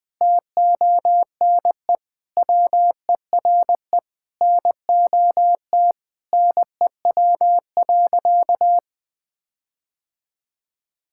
Morse code, heard as a keyed tone, about 20 wpm, 700 Hz. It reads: TONE WERE NOT NEW.
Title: Sentences from Top 1600 Words 20wpm